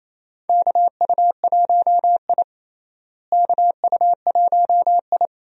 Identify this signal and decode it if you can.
Morse code, heard as a keyed tone, about 28 wpm, 700 Hz. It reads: KU1I KU1I
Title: USA Call Signs - 4 Characters - Repeated 2x - 28wpm